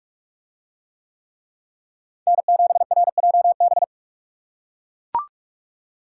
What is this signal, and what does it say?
Morse code, heard as a keyed tone, about 43 wpm, 700 Hz. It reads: N7RJB